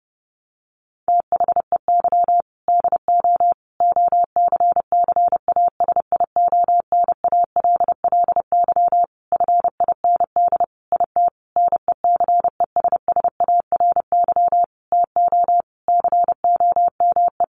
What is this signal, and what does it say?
Morse code, heard as a keyed tone, about 30 wpm, 700 Hz. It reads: THEY DO OCCASIONALLY FIND IT NECESSARY TO COME
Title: Sentences from Top 1300 Words 30wpm